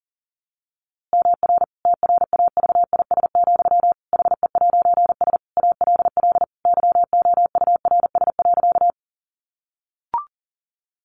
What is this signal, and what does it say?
Morse code, heard as a keyed tone, about 40 wpm, 700 Hz. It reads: MR TRAVIS, HE'S ALL YOURS.
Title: A Sound Of Thunder 40wpm